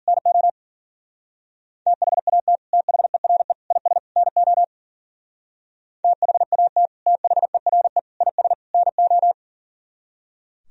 Morse code, heard as a keyed tone, about 51 wpm, 700 Hz. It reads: NO THAT THERE IS NO THAT THERE IS NO